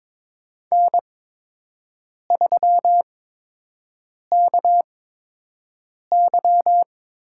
Morse code, heard as a keyed tone, about 23 wpm, spaced as for 6 wpm, 700 Hz. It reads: N3KY
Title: USA Call Signs - 5 Characters - 8x Character Spacing - 22wpm